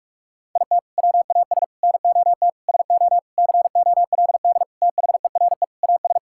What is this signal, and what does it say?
Morse code, heard as a keyed tone, about 47 wpm, 700 Hz. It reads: IT WAS NOT SO COLD THERE AS
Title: Sentences from Top 1600 Words 45wpm